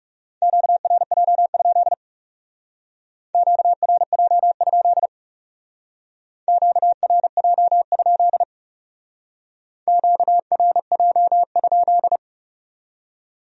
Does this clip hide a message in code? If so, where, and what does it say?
Yes: Morse code from 0.4 to 12.2 s, heard as a keyed tone, about 38 wpm, 700 Hz. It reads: QRJ? QRJ? QRJ? QRJ?